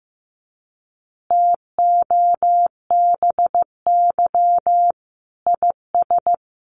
Morse code, heard as a keyed tone, about 15 wpm, 700 Hz. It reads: TOBY IS
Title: Easy Phrases 15wpm